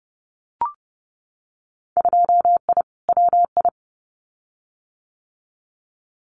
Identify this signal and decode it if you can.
Morse code, heard as a keyed tone, about 30 wpm, 700 Hz. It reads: 2I WI
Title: Winter Field Day Exchanges 30wpm